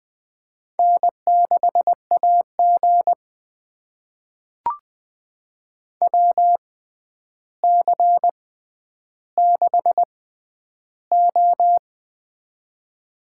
Morse code, heard as a keyed tone, about 20 wpm, spaced as for 10 wpm, 700 Hz. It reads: N6AG WC6O
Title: USA Call Signs - 4 Characters - 6x Character Spacing - 20wpm